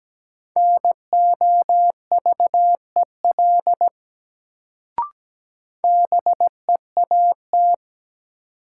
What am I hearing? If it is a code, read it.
Morse code, heard as a keyed tone, about 17 wpm, 700 Hz. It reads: NOVEL BEAT
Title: Sentences from Top 1300 Words - New Word Drill Alt - 17wpm